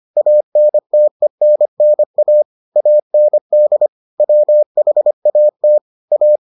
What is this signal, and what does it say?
Morse code, heard as a keyed tone, about 25 wpm, 600 Hz. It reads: ANTENNA AND WHAT A